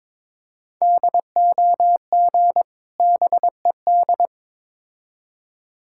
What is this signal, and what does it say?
Morse code, heard as a keyed tone, about 22 wpm, 700 Hz. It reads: DOG BED
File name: Difficult Words with Hints - 22wpm